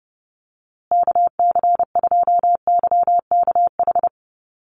Morse code, heard as a keyed tone, about 30 wpm, 700 Hz. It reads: KC2YKH
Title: USA Call Signs Repeated 2x 30wpm